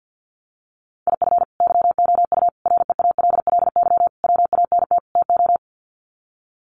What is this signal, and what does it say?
Morse code, heard as a keyed tone, about 50 wpm, 700 Hz. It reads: IF YOU REALLY WANT TO
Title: Sets of 5 Words Encore2 50wpm